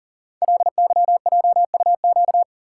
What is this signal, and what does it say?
Morse code, heard as a keyed tone, about 44 wpm, 700 Hz. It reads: LYJUQ
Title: Code Groups - 5 Letters - 40wpm